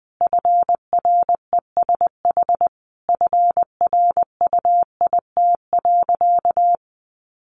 Morse code, heard as a keyed tone, about 20 wpm, 700 Hz. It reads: FRESH FRUIT.
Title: Sentences from Top 1300 Words 20wpm